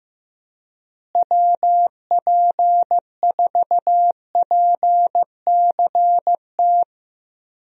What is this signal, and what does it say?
Morse code, heard as a keyed tone, about 16 wpm, 700 Hz. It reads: WP4PCT
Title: USA Call Signs - 6 Characters - Repeated 3x - 15wpm